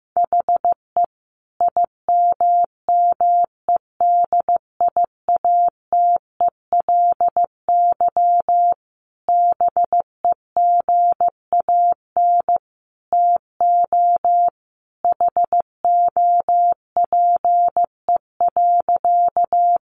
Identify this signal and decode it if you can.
Morse code, heard as a keyed tone, about 15 wpm, 700 Hz. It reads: HE IMMEDIATELY BEGAN TO HOPE.
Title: Sentences from Top 1100 Words 15wpm